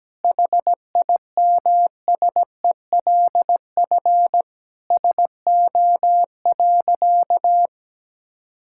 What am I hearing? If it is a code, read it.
Morse code, heard as a keyed tone, about 17 wpm, 700 Hz. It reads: HIMSELF SO.
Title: Sentences from Top 1600 Words 17wpm